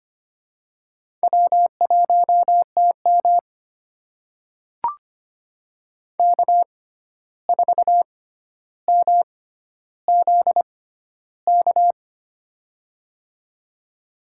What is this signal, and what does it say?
Morse code, heard as a keyed tone, about 25 wpm, spaced as for 12 wpm, 700 Hz. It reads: W1TM K4MZK
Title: USA Call Signs - 6x Character Spacing - 25wpm